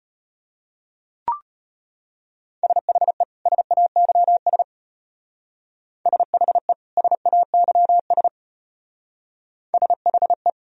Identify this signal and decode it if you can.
Morse code, heard as a keyed tone, about 34 wpm, 700 Hz. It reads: SHE SAYS SHE SAYS SHE